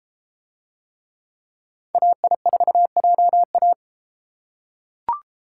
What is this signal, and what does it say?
Morse code, heard as a keyed tone, about 32 wpm, 700 Hz. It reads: AI4JA